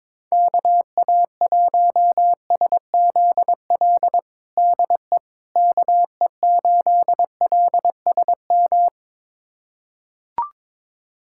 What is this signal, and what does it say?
Morse code, heard as a keyed tone, about 22 wpm, 700 Hz. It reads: KA1SZL DE KE8LSM
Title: USA Callsign DE Callsign - 6 Characters - 22wpm